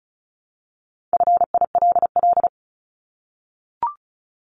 Morse code, heard as a keyed tone, about 34 wpm, 700 Hz. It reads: FILL
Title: Top 500 Words Repeated 35wpm